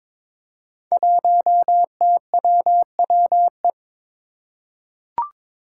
Morse code, heard as a keyed tone, about 22 wpm, 700 Hz. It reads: 1TWWE
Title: Five Challenging Letter-Number Combinations 22wpm